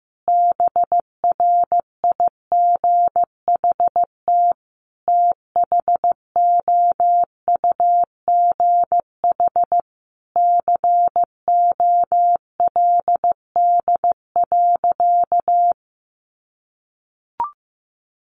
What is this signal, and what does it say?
Morse code, heard as a keyed tone, about 15 wpm, 700 Hz. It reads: BRIGHT THOUGH COLD.
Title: Sentences from Top 1000 Words 15wpm